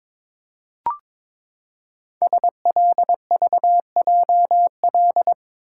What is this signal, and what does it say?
Morse code, heard as a keyed tone, about 22 wpm, 700 Hz. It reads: SLVJL
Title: Code Groups - 5 Letters - 22wpm